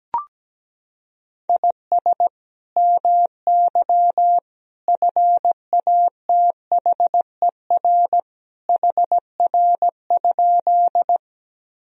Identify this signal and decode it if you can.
Morse code, heard as a keyed tone, about 17 wpm, 700 Hz. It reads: IS MY FATHER HR?
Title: Sentences from Top 300 Words and Abbreviations 17wpm